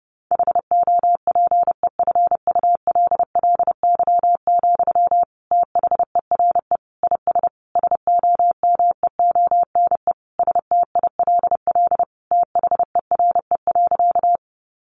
Morse code, heard as a keyed tone, about 30 wpm, 700 Hz. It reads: HOPEFULLY, THERE IS SOMEONE STILL THERE.
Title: Sentences from Top 1500 Words 30wpm